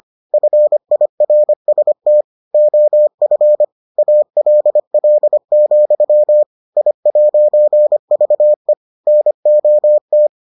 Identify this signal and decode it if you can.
Morse code, heard as a keyed tone, about 25 wpm, 600 Hz. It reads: FIRST OF ALL, I'VE NOT